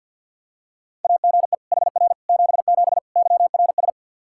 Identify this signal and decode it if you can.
Morse code, heard as a keyed tone, about 49 wpm, 700 Hz. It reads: AGE HR 77 YRS
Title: Practice QSO - Exchange 3 - More Info - 50wpm